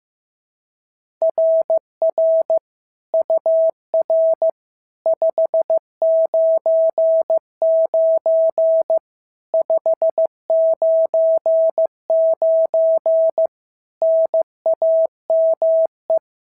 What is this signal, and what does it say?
Morse code, heard as a keyed tone, about 14 wpm, 650 Hz. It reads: RR UR 599 599 NAME
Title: SOTA QSO Practice 15wpm